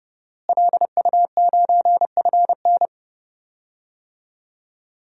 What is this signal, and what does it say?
Morse code, heard as a keyed tone, about 31 wpm, 700 Hz. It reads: LU9FN
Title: DXCC-100 Call Signs - 5 Characters - Repeated 3x - 30wpm